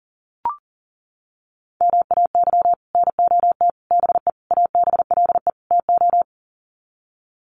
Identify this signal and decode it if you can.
Morse code, heard as a keyed tone, about 40 wpm, 700 Hz. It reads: MAY NOT BE ABLE TO
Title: Sets of 5 Words 40wpm